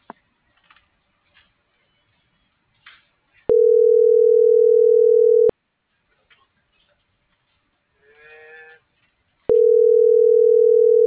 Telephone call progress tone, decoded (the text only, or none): ringback tone